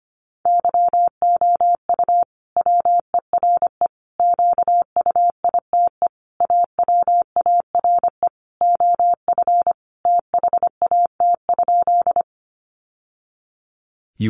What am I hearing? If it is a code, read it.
Morse code, heard as a keyed tone, about 25 wpm, 700 Hz. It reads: YOU WERE QUITE AWARE OF THAT?
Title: Sentences from Top 1200 Words 25wpm